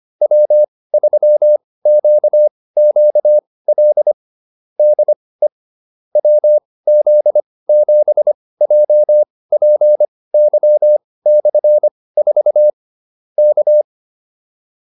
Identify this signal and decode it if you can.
Morse code, heard as a keyed tone, about 25 wpm, spaced as for 18 wpm, 600 Hz. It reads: W3QQL DE WZ7JPY/4 K
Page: intermediate-practice-files